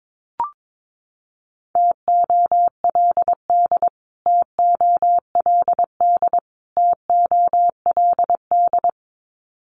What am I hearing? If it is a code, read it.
Morse code, heard as a keyed tone, about 22 wpm, 700 Hz. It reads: TOLD TOLD TOLD
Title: Top 500 Words Repeated 22wpm